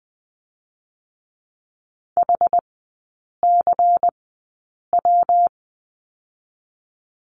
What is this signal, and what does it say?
Morse code, heard as a keyed tone, about 19 wpm, 700 Hz. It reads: H C W
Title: 114 Lesson 19e - Letter w - Three Characters 20wpm